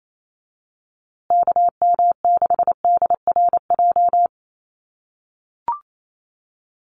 Morse code, heard as a keyed tone, about 28 wpm, 700 Hz. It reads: KM6DRJ
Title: USA Call Signs Repeated 3x 28wpm